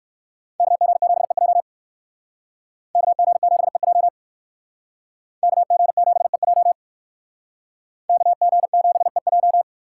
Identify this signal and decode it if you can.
Morse code, heard as a keyed tone, about 57 wpm, 700 Hz. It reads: KG7EJ KG7EJ KG7EJ KG7EJ